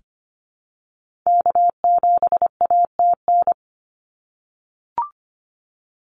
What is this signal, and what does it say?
Morse code, heard as a keyed tone, about 25 wpm, 700 Hz. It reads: K7ATN
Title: NA SOTA Activator Callsigns 25wpm